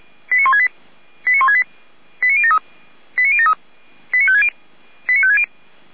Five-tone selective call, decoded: 89168 89168 89073 89073 89560 89560

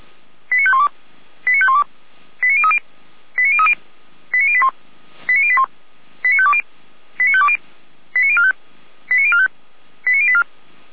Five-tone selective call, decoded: 89512 89512 89030 89030 89081 89081 89520 89520 89045 89045 89084

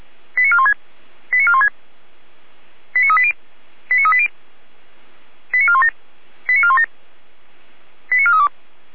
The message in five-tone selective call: 89416 89416 89380 89380 89417 89417 89432

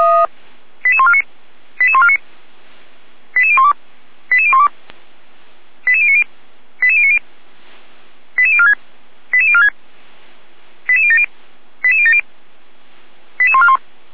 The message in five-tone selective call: 88149 88149 88012 88012 88090 88090 88046 88046 88070 88070 88141

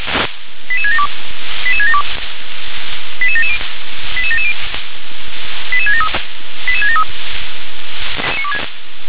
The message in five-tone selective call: 88672 88672 88770 88770 88673 88673

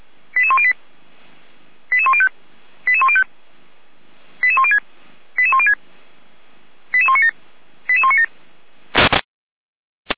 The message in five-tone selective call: 88108 88105 88105 88106 88106 88197 88197

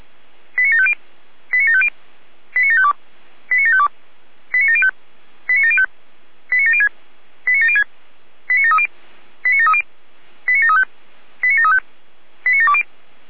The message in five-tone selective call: 89750 89750 89752 89752 89705 89705 89706 89706 89730 89730 89735 89735 89720

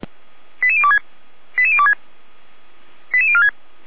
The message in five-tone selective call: 88026 88026 88046

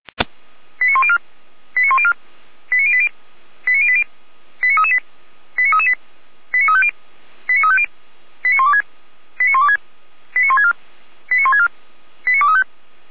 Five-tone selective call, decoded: 89104 89104 89080 89080 89338 89338 89350 89350 89126 89126 89164 89164 89235

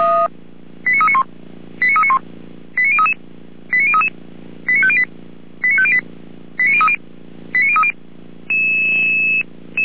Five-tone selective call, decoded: 89391 89391 89033 89033 89508 89508 89030 89030

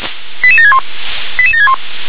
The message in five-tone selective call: 88761 88761

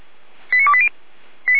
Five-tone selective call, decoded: 89280

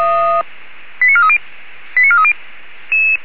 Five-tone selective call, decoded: 89430 89430

